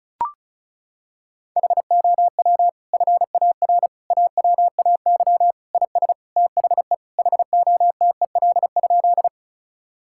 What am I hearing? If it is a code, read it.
Morse code, heard as a keyed tone, about 34 wpm, 700 Hz. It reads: HOW FAR AWAY IS THE HOTEL?
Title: Sentences from Top 1600 Words 35wpm